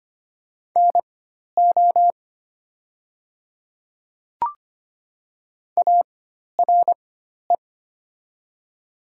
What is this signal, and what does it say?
Morse code, heard as a keyed tone, about 25 wpm, spaced as for 12 wpm, 700 Hz. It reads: NO ARE
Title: 4x Character Spacing - Top 100 Words - 25wpm